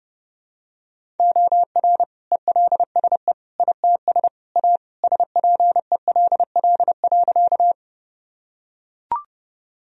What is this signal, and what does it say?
Morse code, heard as a keyed tone, about 30 wpm, 700 Hz. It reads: OR ELSE ITS A SPELL.
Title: Sentences from Top 1000 Words - Review Part 1 - 30wpm